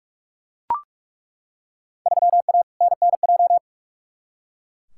Morse code, heard as a keyed tone, about 42 wpm, 700 Hz. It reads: 3A NNJ
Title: Field Day Exchanges - 45wpm